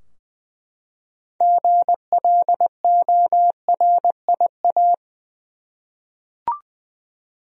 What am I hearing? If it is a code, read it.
Morse code, heard as a keyed tone, about 20 wpm, 700 Hz. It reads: GLORIA
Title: USA Common Names - 20wpm